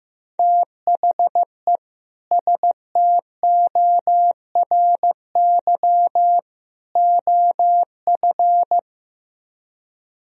Morse code, heard as a keyed tone, about 15 wpm, 700 Hz. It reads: THE STORY OF
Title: Speed-Racing - Sets of 3 Words 15wpm